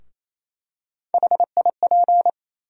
Morse code, heard as a keyed tone, about 28 wpm, 700 Hz. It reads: HIP
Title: USA Call Sign Postfixes - 3 Letters 28wpm